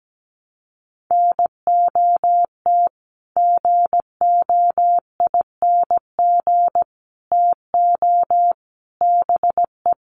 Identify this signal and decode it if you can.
Morse code, heard as a keyed tone, about 17 wpm, 700 Hz. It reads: NOT GOING TO BE
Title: Sets of 5 Words 17wpm